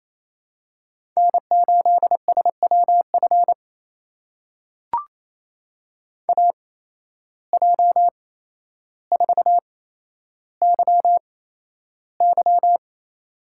Morse code, heard as a keyed tone, about 28 wpm, spaced as for 12 wpm, 700 Hz. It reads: N8SWF AJ4YY
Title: USA Call Signs - 5 Characters - 8x Character Spacing - 28wpm